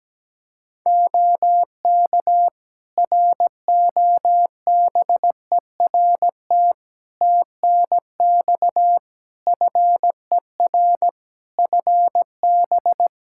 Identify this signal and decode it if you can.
Morse code, heard as a keyed tone, about 17 wpm, 700 Hz. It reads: OK ROBERT TNX FER FB
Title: K1USN SST QSO Practice 17wpm